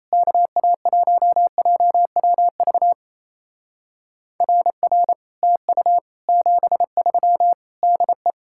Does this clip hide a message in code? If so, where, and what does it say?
Yes: Morse code from 0.1 to 8.3 s, heard as a keyed tone, about 37 wpm, 700 Hz. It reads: KA1JWV RR TU 73 DE